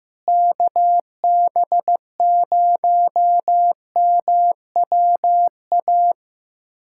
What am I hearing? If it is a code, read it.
Morse code, heard as a keyed tone, about 15 wpm, 700 Hz. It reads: KB0MWA
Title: USA Callsign DE Callsign - 6 Characters - 15wpm